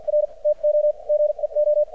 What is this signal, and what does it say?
Morse code, heard as a keyed tone, about 50 wpm, spaced as for 41 wpm, 600 Hz. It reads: R TO GEO
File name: archived / CRSnet